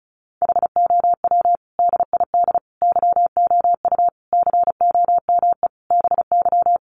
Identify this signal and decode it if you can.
Morse code, heard as a keyed tone, about 34 wpm, 700 Hz. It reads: HOW DID YOU COME BY
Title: Sentences from Top 100 Words 35wpm